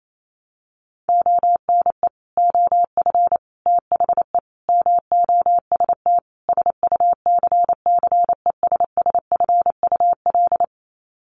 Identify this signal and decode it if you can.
Morse code, heard as a keyed tone, about 28 wpm, 700 Hz. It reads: ONE OF THE MOST SUCCESSFUL
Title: Sets of 5 Words Encore2 28wpm